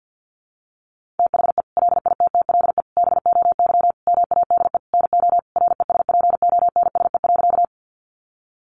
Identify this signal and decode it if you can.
Morse code, heard as a keyed tone, about 50 wpm, 700 Hz. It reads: THE LITTLE BOY MADE NO RESPONSE.
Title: Sentences from Top 1400 Words 50wpm